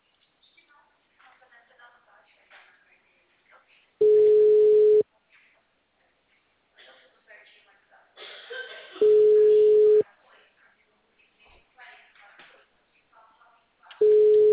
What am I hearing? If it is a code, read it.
Ringback tone (Europe, 425 Hz), with speech.